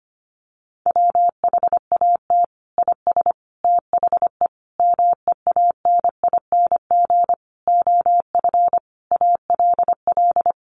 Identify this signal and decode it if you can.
Morse code, heard as a keyed tone, about 25 wpm, 700 Hz. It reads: WHAT IS THE MEANING OF ALL
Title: Sentences from Top 900 Words 25wpm